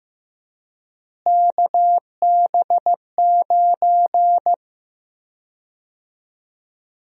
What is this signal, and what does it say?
Morse code, heard as a keyed tone, about 15 wpm, 700 Hz. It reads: KB9
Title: USA Call Sign Prefixes - Repeated 2x - 15wpm